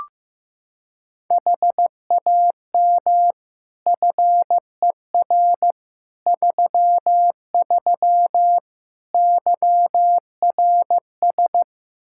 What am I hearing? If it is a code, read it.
Morse code, heard as a keyed tone, about 16 wpm, 700 Hz. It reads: HAM FER 33 YRS
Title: Practice QSO - Exchange 3 - More Info - 15wpm